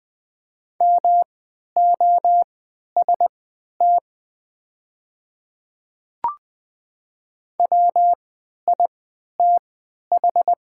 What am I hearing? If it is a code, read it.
Morse code, heard as a keyed tone, about 20 wpm, spaced as for 11 wpm, 690 Hz. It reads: MOST WITH
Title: 3x Character Spacing - Top 100 Words - 20wpm